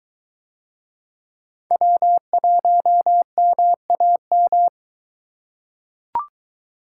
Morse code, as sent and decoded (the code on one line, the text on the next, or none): .-- .---- -- .- --
W1MAM